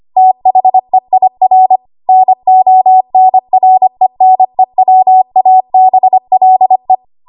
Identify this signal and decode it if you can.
Morse code, heard as a keyed tone, about 25 wpm, 750 Hz. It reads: THEIR NONRENEWABLE
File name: Archive / 25 WPM / 25WPM